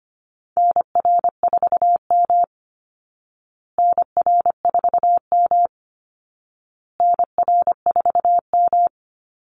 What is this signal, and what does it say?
Morse code, heard as a keyed tone, about 25 wpm, 700 Hz. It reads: NR4M NR4M NR4M
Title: Call Signs Repeated 2x - 25wpm